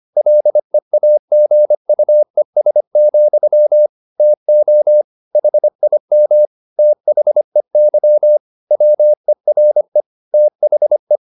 Morse code, heard as a keyed tone, about 25 wpm, 600 Hz. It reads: LEAGUES, TO HIM THEY WERE THE
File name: the old man and the sea part 1 25wpm